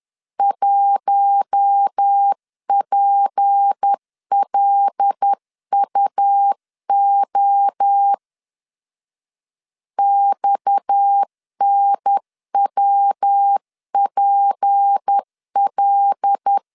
Morse code, heard as a keyed tone, about 10 wpm, 800 Hz. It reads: 1PLUO XNWPL
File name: CW / DET / 1 6 1500C S C 10wpm 4252